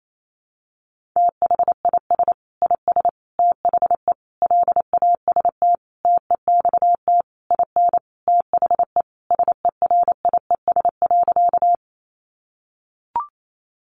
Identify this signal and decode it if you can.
Morse code, heard as a keyed tone, about 28 wpm, 700 Hz. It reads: THIS IS THE LAST TEXT IN THE SERIES.
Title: Sentences from Top 1200 Words 28wpm